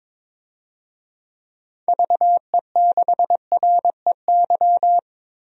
Morse code, heard as a keyed tone, about 22 wpm, 700 Hz. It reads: VE6REY
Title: DXCC-100 Call Signs - 6 Characters - Repeated 3x - 22wpm